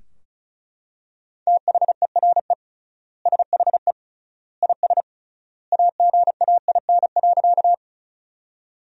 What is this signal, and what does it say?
Morse code, heard as a keyed tone, about 34 wpm, 700 Hz. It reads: THERE SHE IS AGAIN.
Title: 3x Word-Spacing - Sentences from Top 200 Words - 35wpm